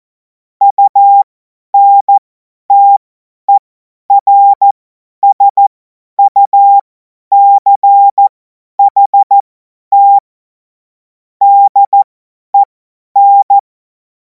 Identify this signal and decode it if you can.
Morse code, heard as a keyed tone, about 14 wpm, spaced as for 10 wpm, 800 Hz. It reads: UNTERSUCHT DEN
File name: Morsekurs / Witze / Witz 18. August 2026 50 F70